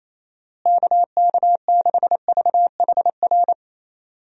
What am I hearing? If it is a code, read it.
Morse code, heard as a keyed tone, about 28 wpm, 700 Hz. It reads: KK6VHR
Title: USA Call Signs - 6 Characters - Repeated 3x - 28wpm